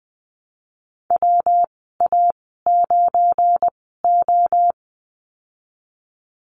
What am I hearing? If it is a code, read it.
Morse code, heard as a keyed tone, about 20 wpm, spaced as for 14 wpm, 700 Hz. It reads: WA9O
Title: 179 Lesson 29g - Slash - Call Signs 20wpm